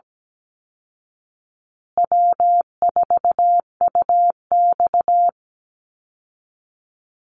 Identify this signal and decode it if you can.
Morse code, heard as a keyed tone, about 17 wpm, 700 Hz. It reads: W4UX
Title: Easy Call Signs Repeated 3x - 17wpm